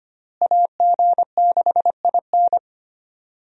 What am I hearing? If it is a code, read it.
Morse code, heard as a keyed tone, about 25 wpm, 700 Hz. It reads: AG6IN
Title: USA Call Signs - 5 Characters - Repeated 3x - 25wpm